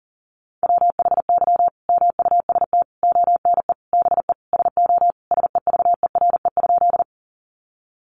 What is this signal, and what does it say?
Morse code, heard as a keyed tone, about 40 wpm, 700 Hz. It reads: WHY MUST ONE BE SO SEVERE?
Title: Sentences from Top 1400 Words 40wpm